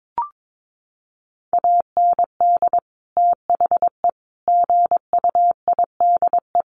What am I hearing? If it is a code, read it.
Morse code, heard as a keyed tone, about 22 wpm, 700 Hz. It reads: AND THE GUIDE
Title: Sentences from Top 600 Words 22wpm